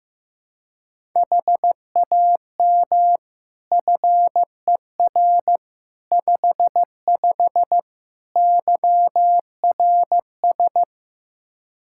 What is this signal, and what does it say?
Morse code, heard as a keyed tone, about 15 wpm, 700 Hz. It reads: HAM FER 55 YRS
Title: Practice QSO - Exchange 3 - More Info - 15wpm